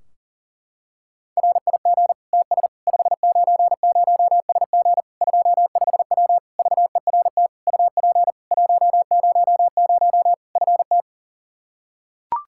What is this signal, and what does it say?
Morse code, heard as a keyed tone, about 40 wpm, 700 Hz. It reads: RIG TS 590SG 25W VERT UP 100 FT
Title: Practice QSO - Exchange 2 - Info - 40wpm